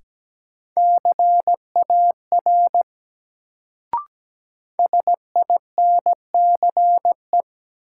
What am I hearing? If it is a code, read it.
Morse code, heard as a keyed tone, about 17 wpm, 700 Hz. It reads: CAR SINCE
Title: Sentences from Top 300 Words - New Word Drill - 17wpm